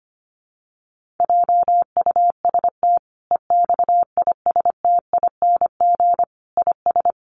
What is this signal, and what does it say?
Morse code, heard as a keyed tone, about 25 wpm, 700 Hz. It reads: JUST EXISTING IS
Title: Sentences from Top 1500 Words 25wpm